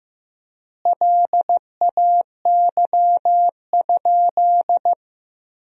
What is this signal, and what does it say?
Morse code, heard as a keyed tone, about 16 wpm, 700 Hz. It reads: LAY?
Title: 4x Word-Spacing - Sentences from Top 400 Words - 15wpm